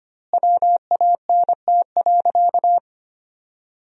Morse code, heard as a keyed tone, about 25 wpm, 700 Hz. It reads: WANT.
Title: 2x Word-Spacing - Sentences from Top 200 Words - 25wpm